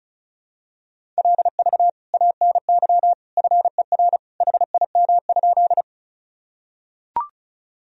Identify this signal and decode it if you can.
Morse code, heard as a keyed tone, about 36 wpm, 700 Hz. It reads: LV ANY FER HIM?
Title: Sentences from Top 400 Words and Abbreviations 35wpm